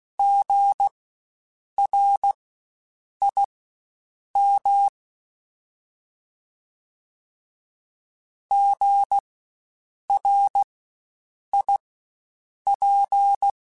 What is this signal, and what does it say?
Morse code, heard as a keyed tone, about 16 wpm, spaced as for 7 wpm, 800 Hz. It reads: GRIM GRIP
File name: Spaced Alphas / Spaced Alphas mp3s / Gs sp @15